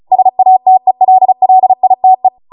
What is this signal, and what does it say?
Morse code, heard as a keyed tone, about 36 wpm, 750 Hz. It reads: SATELLITE